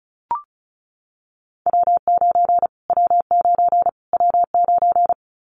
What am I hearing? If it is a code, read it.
Morse code, heard as a keyed tone, about 34 wpm, 700 Hz. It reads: W9 W9 W9
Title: Callsign Prefixes for Top 25 DXCC - Repeated 3x - 35wpm